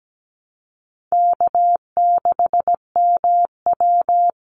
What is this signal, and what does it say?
Morse code, heard as a keyed tone, about 17 wpm, 700 Hz. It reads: K6MW
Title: NA SOTA Chaser Callsigns 17wpm